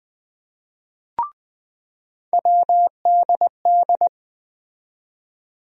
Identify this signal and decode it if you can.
Morse code, heard as a keyed tone, about 20 wpm, 700 Hz. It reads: WDD
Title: USA Call Sign Postfixes - 3 Letters 20wpm